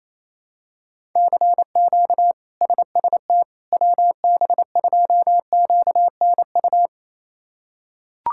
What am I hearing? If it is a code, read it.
Morse code, heard as a keyed tone, about 28 wpm, 700 Hz. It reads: CQ SST WB2QNU
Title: K1USN SST QSO Practice - 28wpm